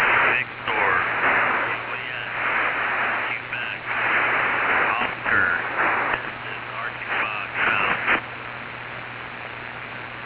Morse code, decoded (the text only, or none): none